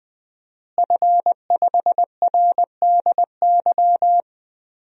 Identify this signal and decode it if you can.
Morse code, heard as a keyed tone, about 20 wpm, 700 Hz. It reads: F5RDY
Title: DXCC-100 Call Signs - 5 Characters - Repeated 3x - 20wpm